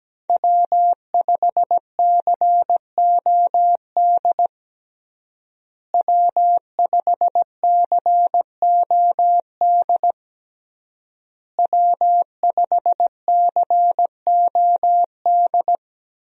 Morse code, heard as a keyed tone, about 17 wpm, 700 Hz. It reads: W5COD W5COD W5COD
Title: USA Call Signs - 5 Characters - Repeated 3x - 17wpm